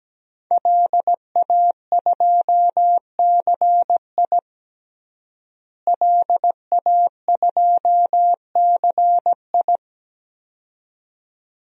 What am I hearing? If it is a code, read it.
Morse code, heard as a keyed tone, about 17 wpm, 700 Hz. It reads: LA2CI LA2CI
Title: DXCC-100 Call Signs - 5 Characters - Repeated 2x - 17wpm